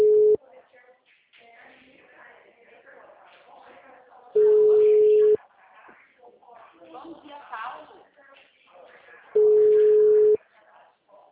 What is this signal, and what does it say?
Ringback tone (Europe, 425 Hz), with speech.